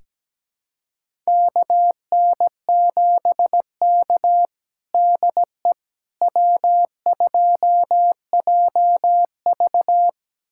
Morse code, heard as a keyed tone, about 17 wpm, 700 Hz. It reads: KN7K DE W2JV
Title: USA Callsign DE Callsign - 4 Characters - 17wpm